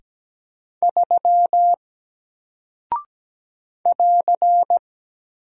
Morse code, heard as a keyed tone, about 17 wpm, 700 Hz. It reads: 3 <AR>
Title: Single Character (Letter:Number:Punctuation:Prosign) Alt 17wpm